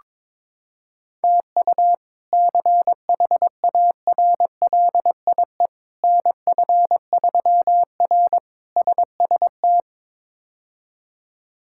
Morse code, heard as a keyed tone, about 22 wpm, 700 Hz. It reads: TU CHARLIE NF3R SST
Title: K1USN SST QSO Practice - 22wpm